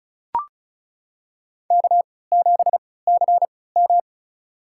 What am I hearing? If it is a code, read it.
Morse code, heard as a keyed tone, about 34 wpm, spaced as for 20 wpm, 700 Hz. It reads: K7CM
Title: USA Call Signs - 4 Characters - 3x Character Spacing - 35wpm